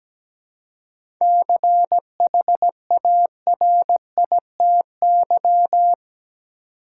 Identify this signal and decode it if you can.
Morse code, heard as a keyed tone, about 17 wpm, 700 Hz. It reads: CHARITY
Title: Seven Letter Words - 17wpm